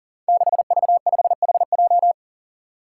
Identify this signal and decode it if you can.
Morse code, heard as a keyed tone, about 40 wpm, 700 Hz. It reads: 6V5HJ
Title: Five Challenging Letter-Number Combinations 40wpm